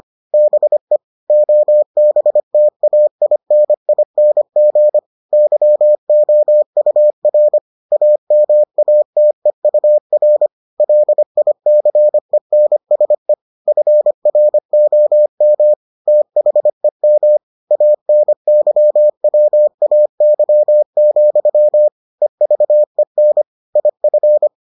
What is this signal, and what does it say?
Morse code, heard as a keyed tone, about 25 wpm, 600 Hz. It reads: BE OBTAINING YOUR AMATEUR LICENSE FROM THEM ANYWAY, EVEN IF